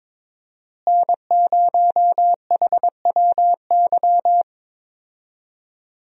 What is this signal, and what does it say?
Morse code, heard as a keyed tone, about 21 wpm, 700 Hz. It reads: N0HWY
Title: USA Call Signs - 5 Characters - Repeated 3x - 22wpm